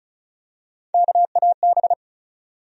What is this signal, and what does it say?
Morse code, heard as a keyed tone, about 36 wpm, 700 Hz. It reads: KAB
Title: USA Call Sign Postfixes - 3 Letters 35wpm